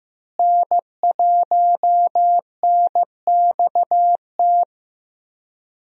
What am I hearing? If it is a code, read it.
Morse code, heard as a keyed tone, about 15 wpm, 700 Hz. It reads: N1NXT
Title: USA Call Signs - 5 Characters - Repeated 2x - 15wpm